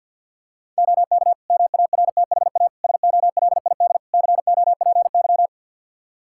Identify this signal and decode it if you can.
Morse code, heard as a keyed tone, about 50 wpm, 700 Hz. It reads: OK MARTHA SOLID COPY
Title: Practice QSO - Exchange 3 - More Info - 50wpm